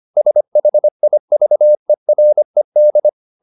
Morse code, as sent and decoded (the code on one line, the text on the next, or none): ... .... .. ...- . .-. -..
SHIVERED